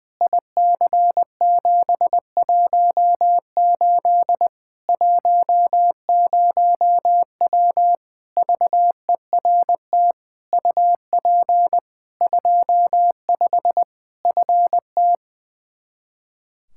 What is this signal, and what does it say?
Morse code, heard as a keyed tone, about 20 wpm, 700 Hz. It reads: IC718 10W VERT UP 25 FT
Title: Practice QSO - Exchange 2 - Info - 20wpm